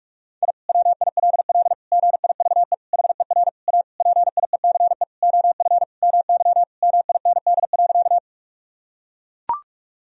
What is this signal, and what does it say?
Morse code, heard as a keyed tone, about 41 wpm, 700 Hz. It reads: I WILL GIVE HER A PIECE OF MY MIND.
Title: Sentences from Top 1000 Words - Review Part 2 - 45wpm